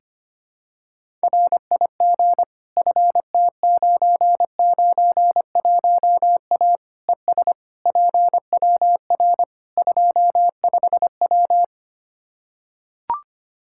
Morse code, heard as a keyed tone, about 25 wpm, 700 Hz. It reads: RIG FT991A ES PWR 25W
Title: Practice QSO - Exchange 2 - Info - 25wpm